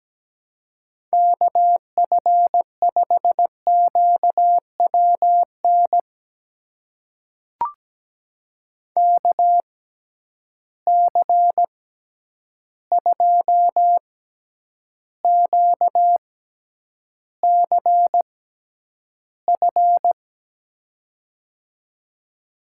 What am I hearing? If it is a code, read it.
Morse code, heard as a keyed tone, about 17 wpm, spaced as for 9 wpm, 700 Hz. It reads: KF5QWN KC2QCF
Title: USA Call Signs - 6 Characters - 6x Character Spacing - 17wpm